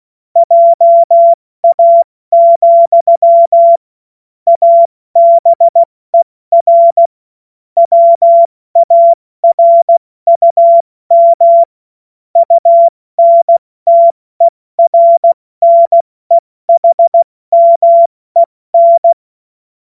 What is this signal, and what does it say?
Morse code, heard as a keyed tone, about 16 wpm, 680 Hz. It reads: JA, ABER WARUM UNTERNEHMEN
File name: Morsekurs / Witze / Witz 13. Dezember 2022 70 F80